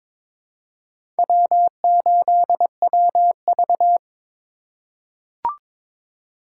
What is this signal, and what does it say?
Morse code, heard as a keyed tone, about 22 wpm, 700 Hz. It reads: W8WV
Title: USA Call Signs - 4x Character Spacing - 22wpm